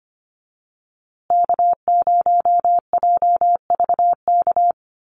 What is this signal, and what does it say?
Morse code, heard as a keyed tone, about 25 wpm, 700 Hz. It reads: K0JVK
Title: USA Call Signs Repeated 3x 25wpm